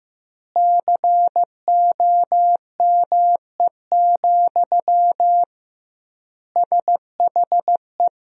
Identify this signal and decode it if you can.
Morse code, heard as a keyed tone, about 16 wpm, 700 Hz. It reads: COME, SHE
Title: 2x Word-Spacing - Sentences from Top 100 Words - 15wpm